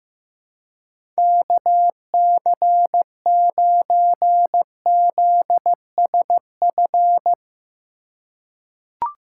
Morse code, heard as a keyed tone, about 15 wpm, 700 Hz. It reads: KC9ZSF
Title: USA Call Signs - 6 Characters - 2x Character Spacing - 15wpm